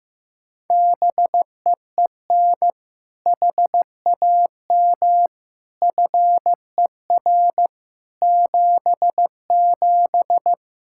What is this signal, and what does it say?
Morse code, heard as a keyed tone, about 16 wpm, 700 Hz. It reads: BEEN HAM FER 77
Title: Practice QSO - Exchange 3 - More Info - 15wpm